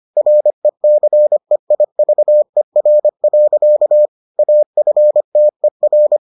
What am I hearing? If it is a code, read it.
Morse code, heard as a keyed tone, about 25 wpm, 600 Hz. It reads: RECEIVER. AFTER